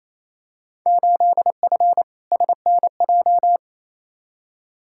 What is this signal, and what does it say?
Morse code, heard as a keyed tone, about 28 wpm, 700 Hz. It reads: 8F SNJ
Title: Field Day Exchanges Encore - 28wpm